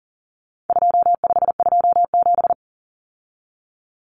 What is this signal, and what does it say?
Morse code, heard as a keyed tone, about 40 wpm, 700 Hz. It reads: 2527
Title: Numbers - 4-Digit 40wpm